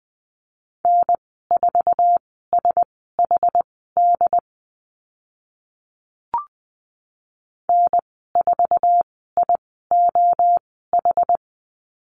Morse code, heard as a keyed tone, about 20 wpm, spaced as for 14 wpm, 700 Hz. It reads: N4SHD N4IOH
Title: 069 Lesson 12g - Letter d - Call Signs 20wpm